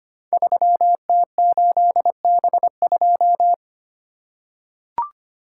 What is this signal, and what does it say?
Morse code, heard as a keyed tone, about 25 wpm, 700 Hz. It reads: 3T8B2
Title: Code Groups - 5 Letter:Numbers - 25wpm